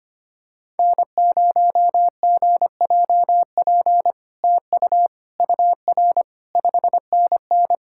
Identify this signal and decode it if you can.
Morse code, heard as a keyed tone, about 25 wpm, 700 Hz. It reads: N0GJP TU UR 5NN
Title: SOTA QSO Practice 25wpm